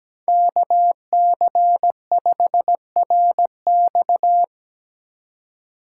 Morse code, heard as a keyed tone, about 20 wpm, 700 Hz. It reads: KC5RX
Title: Speed Racing - USA 2x2 Call Signs 17wpm